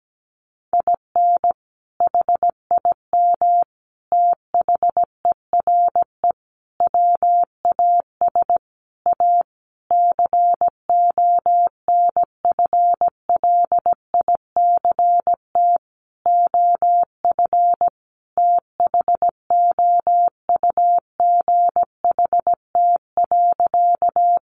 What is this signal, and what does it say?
Morse code, heard as a keyed tone, about 17 wpm, 700 Hz. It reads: IN HIM THERE WAS A CONFLICT OF THOUGHT.
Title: Sentences from Top 1300 Words 17wpm